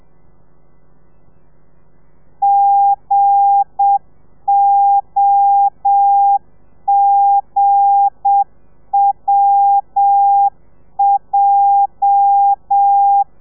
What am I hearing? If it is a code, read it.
Morse code, heard as a keyed tone, about 7 wpm, 800 Hz. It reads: GOGWJ
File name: telegrafia / i2tfj / Lez8 35